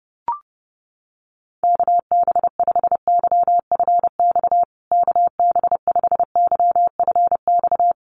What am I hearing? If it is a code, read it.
Morse code, heard as a keyed tone, about 30 wpm, 700 Hz. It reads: KB5YFX KB5YFX
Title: USA Call Signs Repeated 3x 30wpm